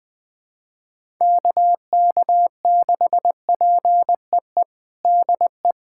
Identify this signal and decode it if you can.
Morse code, heard as a keyed tone, about 21 wpm, 700 Hz. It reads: KK6PEE DE
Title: USA Callsign DE Callsign - 6 Characters - 20wpm